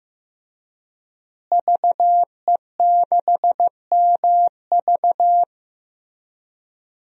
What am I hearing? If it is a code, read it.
Morse code, heard as a keyed tone, about 16 wpm, 700 Hz. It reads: VE6MV
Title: DXCC-100 Call Signs - 5 Characters - Repeated 3x - 15wpm